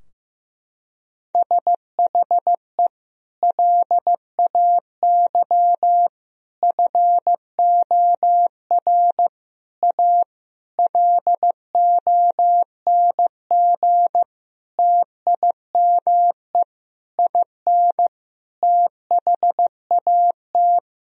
Morse code, heard as a keyed tone, about 16 wpm, 700 Hz. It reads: SHE LAY FOR A LONG TIME IN THAT
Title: Sentences from Top 1000 Words - Review Part 2 - 15wpm